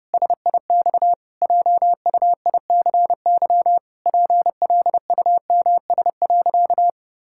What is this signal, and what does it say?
Morse code, heard as a keyed tone, about 30 wpm, 700 Hz. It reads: SIX JUICY PLUMS.
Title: Pangram Challenge v2 - 30wpm